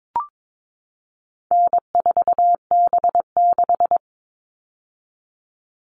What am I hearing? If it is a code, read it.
Morse code, heard as a keyed tone, about 22 wpm, 700 Hz. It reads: N4B6
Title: All Cut Numbers - 4-Digit 22wpm